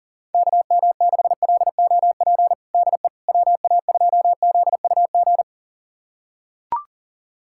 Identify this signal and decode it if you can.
Morse code, heard as a keyed tone, about 40 wpm, 700 Hz. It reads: KM6LOP DE WA2ZUG
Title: USA Callsign DE Callsign - 6 Characters - 40wpm